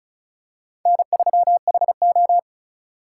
Morse code, heard as a keyed tone, about 34 wpm, 700 Hz. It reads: N3HO
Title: USA Call Signs - 3x Character Spacing - 35wpm